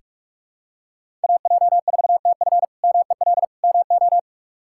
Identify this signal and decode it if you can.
Morse code, heard as a keyed tone, about 46 wpm, 700 Hz. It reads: AJ4TF MEL MO
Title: K1USN SST QSO Practice - 45wpm